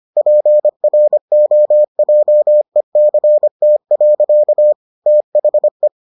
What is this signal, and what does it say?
Morse code, heard as a keyed tone, about 25 wpm, 600 Hz. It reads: PROJECT. THE